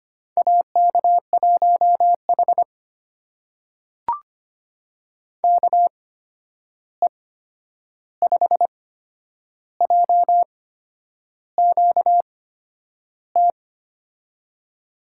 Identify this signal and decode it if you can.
Morse code, heard as a keyed tone, about 25 wpm, spaced as for 9 wpm, 700 Hz. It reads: AK1H KE5JQT
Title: USA Call Signs - 8x Character Spacing - 25wpm